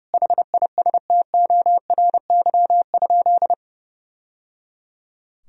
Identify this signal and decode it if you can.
Morse code, heard as a keyed tone, about 30 wpm, 700 Hz. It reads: HISTORY?
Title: Sentences from Top 1000 Words - Review Part 4 - 30wpm